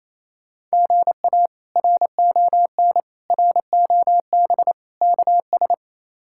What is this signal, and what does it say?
Morse code, heard as a keyed tone, about 28 wpm, 700 Hz. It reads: GA RON ROB KS
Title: K1USN SST QSO Practice - 28wpm